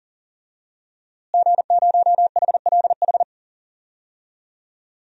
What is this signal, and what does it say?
Morse code, heard as a keyed tone, about 40 wpm, 700 Hz. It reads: G0HLH
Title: DXCC-100 Call Signs - 5 Characters - Repeated 3x - 40wpm